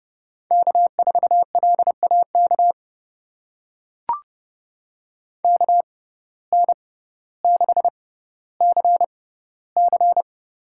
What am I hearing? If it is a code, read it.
Morse code, heard as a keyed tone, about 30 wpm, spaced as for 15 wpm, 700 Hz. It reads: K4LAK KN6CC
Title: USA Call Signs - 5 Characters - 6x Character Spacing - 30wpm